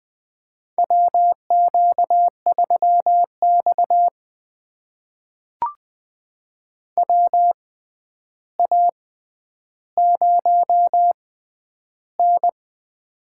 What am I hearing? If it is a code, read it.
Morse code, heard as a keyed tone, about 20 wpm, spaced as for 10 wpm, 700 Hz. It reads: WQ3X WA0N
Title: USA Call Signs - 4 Characters - 6x Character Spacing - 20wpm